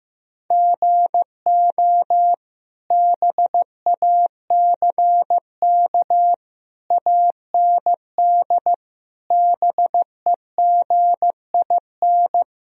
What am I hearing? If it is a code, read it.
Morse code, heard as a keyed tone, about 15 wpm, 700 Hz. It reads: GO BACK AND BEGIN
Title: Sentences from Top 1000 Words - Review Part 1 - 15wpm